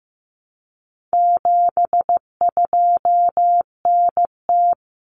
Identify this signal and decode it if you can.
Morse code, heard as a keyed tone, about 15 wpm, 700 Hz. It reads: 72NT
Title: Common Cut Numbers - 4-Digit 15wpm